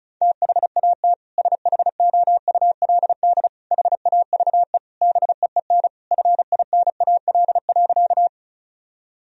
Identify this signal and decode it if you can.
Morse code, heard as a keyed tone, about 34 wpm, 700 Hz. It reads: THAT SHOULD HAVE BEEN FINAL.
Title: Sentences from Top 1000 Words - Review Part 2 - 35wpm